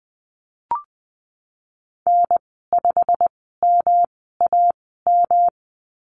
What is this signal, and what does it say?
Morse code, heard as a keyed tone, about 20 wpm, spaced as for 15 wpm, 700 Hz. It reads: N5MAM
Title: 173 Lesson 28g - Number 9 - Call Signs 20wpm